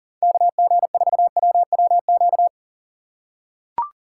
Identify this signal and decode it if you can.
Morse code, heard as a keyed tone, about 39 wpm, 700 Hz. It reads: KG4WWQ
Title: USA Call Signs - 3x Character Spacing - 40wpm